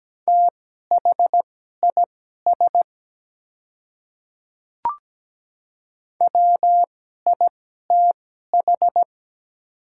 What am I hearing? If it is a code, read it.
Morse code, heard as a keyed tone, about 17 wpm, spaced as for 12 wpm, 700 Hz. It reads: THIS WITH
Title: 2x Character Spacing - Top 100 Words - 17wpm